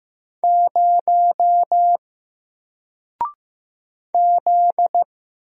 Morse code, heard as a keyed tone, about 15 wpm, 700 Hz. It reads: 0 Z
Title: Single Letter-Number Alt 15wpm